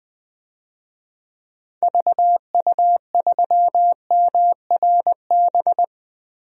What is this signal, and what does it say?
Morse code, heard as a keyed tone, about 21 wpm, 700 Hz. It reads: VU3MRB
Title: DXCC-100 Call Signs - 6 Characters - 20wpm